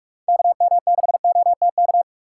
Morse code, heard as a keyed tone, about 41 wpm, 700 Hz. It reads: KM6OTK
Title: USA Call Signs - 6 Characters - 3x Character Spacing - 45wpm